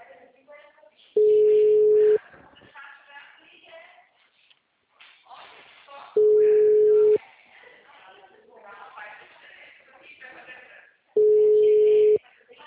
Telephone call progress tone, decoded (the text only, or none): ringback tone